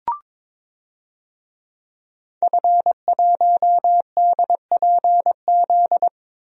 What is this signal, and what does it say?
Morse code, heard as a keyed tone, about 22 wpm, 700 Hz. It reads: F1DPZ
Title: DXCC-100 Call Signs - 5 Characters - Repeated 3x - 22wpm